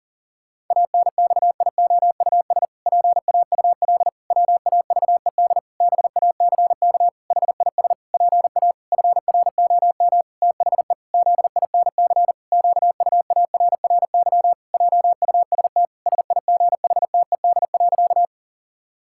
Morse code, heard as a keyed tone, about 40 wpm, 700 Hz. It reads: ANXIOUS PAUL WAVED BACK HIS PA FROM THE ZINC QUARRY JUST SIGHTED.
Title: Pangram Challenge v2 - 40wpm